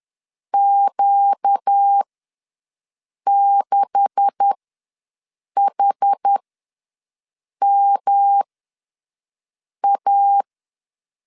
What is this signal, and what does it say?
Morse code, heard as a keyed tone, about 10 wpm, spaced as for 5 wpm, 800 Hz. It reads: Q6HMA
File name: LEZIONI / 10 SIGNS / 10C DOT ws 10wpm 4213